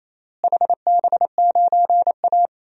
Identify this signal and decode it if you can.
Morse code, heard as a keyed tone, about 28 wpm, 700 Hz. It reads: HB9A
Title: DXCC-100 Call Signs - 4 Characters - Repeated 2x - 28wpm